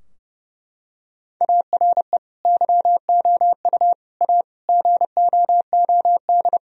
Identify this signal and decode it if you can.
Morse code, heard as a keyed tone, about 30 wpm, 700 Hz. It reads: ARE YOU A GOOD
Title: Sentences from Top 1000 Words - Review Part 4 - 30wpm